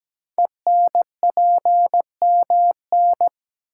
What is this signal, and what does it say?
Morse code, heard as a keyed tone, about 17 wpm, 700 Hz. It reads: ENPMN
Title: Code Groups - 5 Letter:Numbers - 17wpm